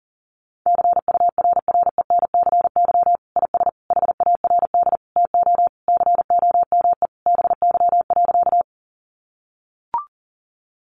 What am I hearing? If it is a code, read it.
Morse code, heard as a keyed tone, about 40 wpm, 700 Hz. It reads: CURRENCY IS HARD TO COME BY.
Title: Sentences from Top 1500 Words 40wpm